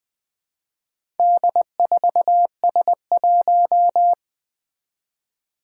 Morse code, heard as a keyed tone, about 20 wpm, 700 Hz. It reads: D4S1
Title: Four Challenging Letter-Number Combinations 20wpm